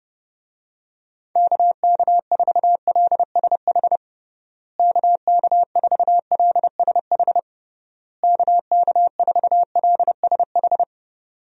Morse code, heard as a keyed tone, about 30 wpm, 700 Hz. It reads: KK4LSH KK4LSH KK4LSH
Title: USA Call Signs - 6 Characters - Repeated 3x - 30wpm